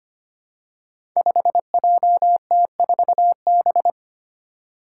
Morse code, heard as a keyed tone, about 25 wpm, 700 Hz. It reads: 5JT4B
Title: Five Challenging Letter-Number Combinations 25wpm